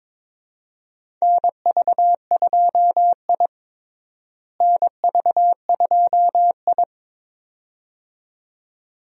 Morse code, heard as a keyed tone, about 22 wpm, 700 Hz. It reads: NV2I NV2I
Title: USA Call Signs - 4 Characters - Repeated 2x - 22wpm